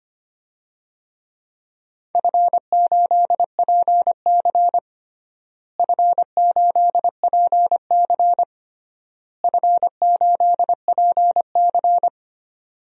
Morse code, heard as a keyed tone, about 25 wpm, 700 Hz. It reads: F8PC F8PC F8PC
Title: DXCC-100 Call Signs - 4 Characters - Repeated 3x - 25wpm